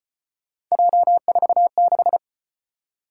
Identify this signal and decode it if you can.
Morse code, heard as a keyed tone, about 34 wpm, 700 Hz. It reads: J46